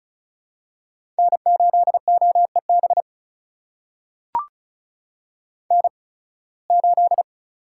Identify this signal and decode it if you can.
Morse code, heard as a keyed tone, about 36 wpm, spaced as for 23 wpm, 700 Hz. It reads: N8OEB N8
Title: USA Call Signs - 5 Characters - 8x Character Spacing - 35wpm